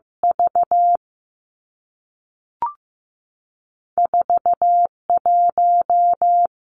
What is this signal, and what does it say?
Morse code, heard as a keyed tone, about 15 wpm, 700 Hz. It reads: V 4 1
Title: Two Challenging Letter-Number Combinations 15wpm